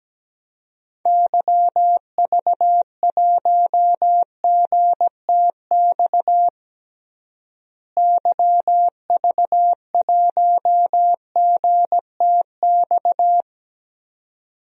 Morse code, heard as a keyed tone, about 17 wpm, 700 Hz. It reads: YV1GTX YV1GTX
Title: DXCC-100 Call Signs - 6 Characters - Repeated 2x - 17wpm